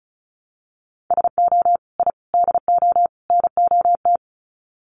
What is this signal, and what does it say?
Morse code, heard as a keyed tone, about 34 wpm, 700 Hz. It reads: SO I DO NOT
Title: Sets of 4 Words 35wpm